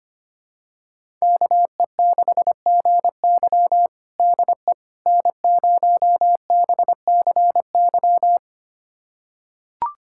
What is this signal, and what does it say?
Morse code, heard as a keyed tone, about 25 wpm, 700 Hz. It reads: KE6GY DE N0BCY
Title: USA Callsign DE Callsign 25wpm